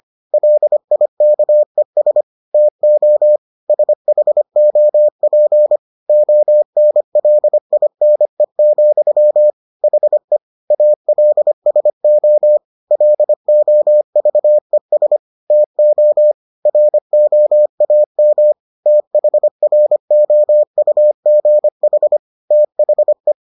Morse code, heard as a keyed tone, about 25 wpm, 600 Hz. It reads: LIKES TO SHOP ONLINE, HE ALSO LOVES TO ROAM THROUGH THE